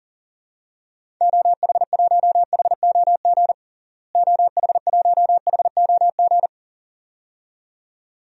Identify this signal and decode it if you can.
Morse code, heard as a keyed tone, about 40 wpm, 700 Hz. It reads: OH1HOG OH1HOG
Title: DXCC-100 Call Signs - 6 Characters - Repeated 2x - 40wpm